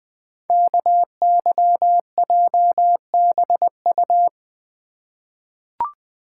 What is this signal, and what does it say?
Morse code, heard as a keyed tone, about 20 wpm, 700 Hz. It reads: KYJBU
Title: Code Groups - 5 Letters - 20wpm